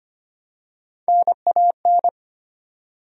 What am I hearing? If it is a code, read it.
Morse code, heard as a keyed tone, about 25 wpm, 700 Hz. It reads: NAN
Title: USA Common Names - Max 3 Letters - 25wpm